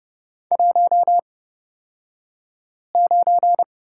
Morse code, heard as a keyed tone, about 30 wpm, spaced as for 5 wpm, 700 Hz. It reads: 19
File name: Lesson 36d - BT - Two Characters 30wpm